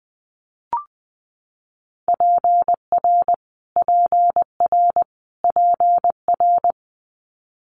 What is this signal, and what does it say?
Morse code, heard as a keyed tone, about 20 wpm, 700 Hz. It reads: PR PR PR
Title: Multi-letter Phonemes Repeated 3x 20wpm